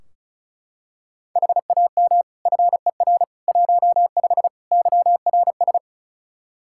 Morse code, heard as a keyed tone, about 36 wpm, 700 Hz. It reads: HAM FER 15 YRS
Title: Practice QSO - Exchange 3 - More Info - 35wpm